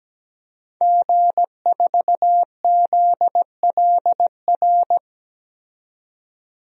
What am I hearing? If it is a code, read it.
Morse code, heard as a keyed tone, about 17 wpm, 700 Hz. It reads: G4ZLR
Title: DXCC-100 Call Signs - 5 Characters - 17wpm